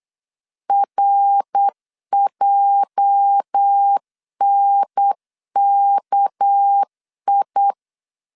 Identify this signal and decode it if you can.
Morse code, heard as a keyed tone, about 8 wpm, 800 Hz. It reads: RJNKI
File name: CW / DET / 1 9 1500C S C 8wpm 5653